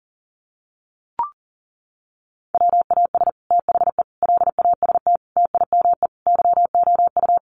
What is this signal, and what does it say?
Morse code, heard as a keyed tone, about 40 wpm, 700 Hz. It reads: WAS THE LAST TIME YOU
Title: Sets of 5 Words Encore2 40wpm